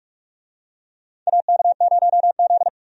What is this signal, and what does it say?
Morse code, heard as a keyed tone, about 46 wpm, 700 Hz. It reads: AK0Z
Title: USA Call Signs - 4 Characters - Repeated 2x - 45wpm